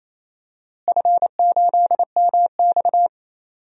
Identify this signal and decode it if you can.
Morse code, heard as a keyed tone, about 28 wpm, 700 Hz. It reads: F8MX
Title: DXCC-100 Call Signs - 4 Characters - Repeated 3x - 28wpm